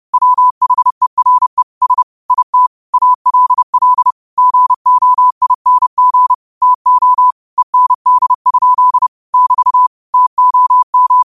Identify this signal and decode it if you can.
Morse code, heard as a keyed tone, about 30 wpm, 1000 Hz. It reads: WHERE S IT ALL GOING TO END? <BT> TOM
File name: qotdfiles / 30wpm